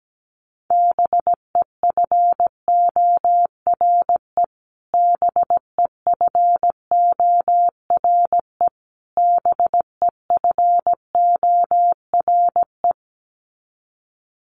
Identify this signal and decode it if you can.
Morse code, heard as a keyed tone, about 17 wpm, 700 Hz. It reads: BEFORE BEFORE BEFORE
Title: Top 500 Words Repeated 17wpm